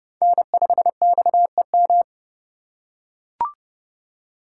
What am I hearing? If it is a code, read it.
Morse code, heard as a keyed tone, about 29 wpm, 700 Hz. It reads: N5XEM